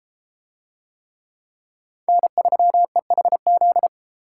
Morse code, heard as a keyed tone, about 32 wpm, 700 Hz. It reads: N3EHZ